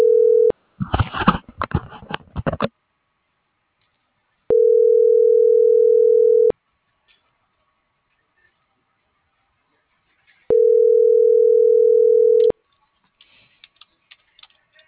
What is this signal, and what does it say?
Ringback tone (North America), 440 + 480 Hz.